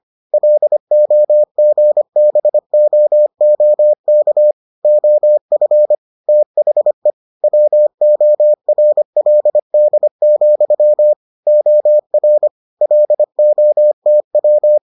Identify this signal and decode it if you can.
Morse code, heard as a keyed tone, about 25 wpm, 600 Hz. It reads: LOGBOOK OF THE WORLD, OR LOTW